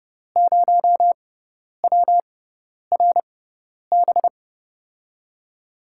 Morse code, heard as a keyed tone, about 30 wpm, spaced as for 10 wpm, 700 Hz. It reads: 0WRB
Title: USA Call Signs - 5 Characters - 6x Character Spacing - 30wpm